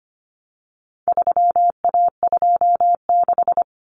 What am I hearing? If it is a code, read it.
Morse code, heard as a keyed tone, about 25 wpm, 700 Hz. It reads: 3A26
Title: Common Cut Numbers - 4-Digit 25wpm